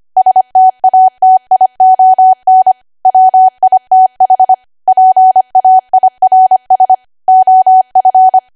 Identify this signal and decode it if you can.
Morse code, heard as a keyed tone, about 25 wpm, 750 Hz. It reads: STATION WITH PAIRS OF
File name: cwa / W1AW-Level3 / 25WPM